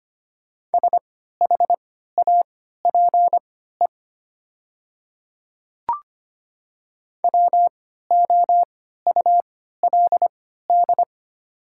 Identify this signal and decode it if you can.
Morse code, heard as a keyed tone, about 25 wpm, spaced as for 14 wpm, 690 Hz. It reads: SHAPE WOULD
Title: 3x Character Spacing - Top 500 Words - 25wpm